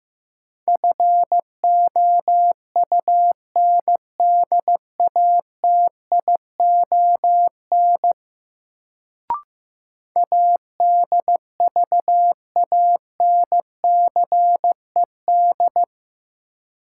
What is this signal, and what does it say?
Morse code, heard as a keyed tone, about 15 wpm, 700 Hz. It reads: FOUNDATION ADVANCED
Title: Sentences from Top 1200 Words - New Word Drill Alt - 15wpm